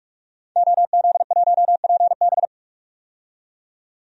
Morse code, heard as a keyed tone, about 47 wpm, 700 Hz. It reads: OZ1PB
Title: DXCC-100 Call Signs - 5 Characters - 45wpm